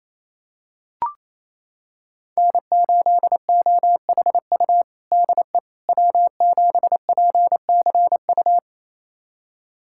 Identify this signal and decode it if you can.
Morse code, heard as a keyed tone, about 28 wpm, 700 Hz. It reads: N8OHU DE W7PCU
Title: USA Callsign DE Callsign - 5 Characters - 28wpm